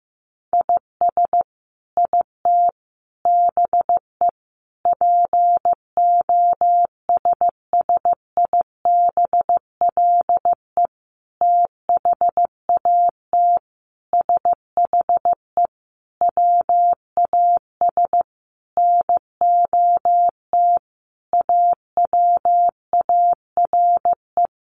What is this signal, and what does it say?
Morse code, heard as a keyed tone, about 15 wpm, 700 Hz. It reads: IS IT BE POSSIBLE THAT SHE WAS NOT AWARE
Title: Sentences from Top 1200 Words 15wpm